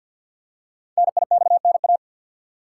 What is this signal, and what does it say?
Morse code, heard as a keyed tone, about 48 wpm, 700 Hz. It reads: NIXNA
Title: Code Groups - 5 Letters - 50wpm